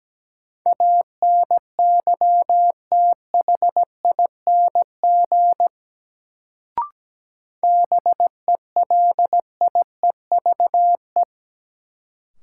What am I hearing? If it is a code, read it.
Morse code, heard as a keyed tone, about 17 wpm, 700 Hz. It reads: ANYTHING BELIEVE
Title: Sets of 4 Words - New Word Drill - 17wpm